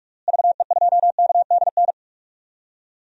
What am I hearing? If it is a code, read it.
Morse code, heard as a keyed tone, about 43 wpm, 700 Hz. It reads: VE2KDN
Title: DXCC-100 Call Signs - 6 Characters - Repeated 3x - 45wpm